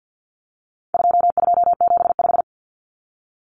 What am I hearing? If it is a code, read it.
Morse code, heard as a keyed tone, about 48 wpm, 700 Hz. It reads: 2275
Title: Common Cut Numbers - 4-Digit 50wpm